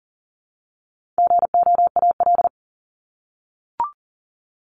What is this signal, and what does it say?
Morse code, heard as a keyed tone, about 40 wpm, 700 Hz. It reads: GOAL
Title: Top 751-1000 Words 40wpm